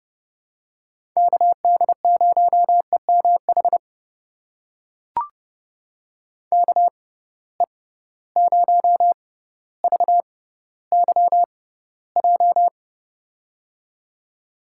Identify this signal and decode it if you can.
Morse code, heard as a keyed tone, about 30 wpm, spaced as for 15 wpm, 700 Hz. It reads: KD0EMH KE0VYJ
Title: USA Call Signs - 6 Characters - 6x Character Spacing - 30wpm